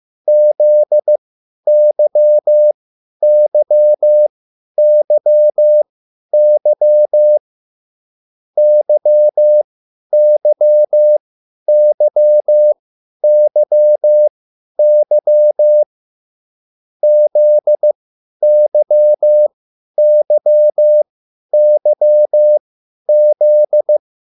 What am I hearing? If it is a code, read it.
Morse code, heard as a keyed tone, about 15 wpm, spaced as for 10 wpm, 600 Hz. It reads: ZYYYY YYYYY ZYYYZ
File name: QSL / CW lesson